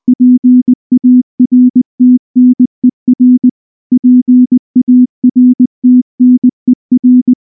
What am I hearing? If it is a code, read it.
Morse code, heard as a keyed tone, about 20 wpm, 250 Hz. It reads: PARTNER PARTNER